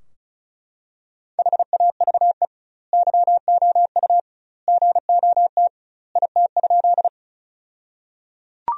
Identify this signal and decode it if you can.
Morse code, heard as a keyed tone, about 36 wpm, 700 Hz. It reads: HAVE YOU GOT IT?
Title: 2x Word-Spacing - Sentences from Top 300 Words - 35wpm